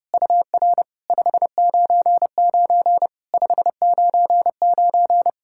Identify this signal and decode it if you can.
Morse code, heard as a keyed tone, about 30 wpm, 690 Hz. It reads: UR 599 599
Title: SOTA QSO Practice 30wpm